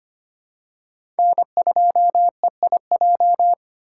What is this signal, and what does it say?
Morse code, heard as a keyed tone, about 25 wpm, 700 Hz. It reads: N2EIJ
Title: USA Call Signs - 5 Characters - 6x Character Spacing - 25wpm